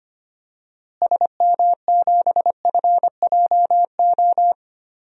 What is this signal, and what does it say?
Morse code, heard as a keyed tone, about 25 wpm, 700 Hz. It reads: SM7FJO
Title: DXCC-100 Call Signs - 6 Characters - Repeated 2x - 25wpm